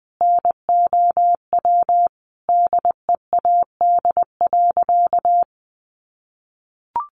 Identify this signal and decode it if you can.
Morse code, heard as a keyed tone, about 20 wpm, 700 Hz. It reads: NOW DEAD.
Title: Sentences from Top 1500 Words 20wpm